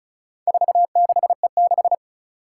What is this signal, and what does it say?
Morse code, heard as a keyed tone, about 34 wpm, 700 Hz. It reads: 46E6
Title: Four Challenging Letter-Number Combinations 35wpm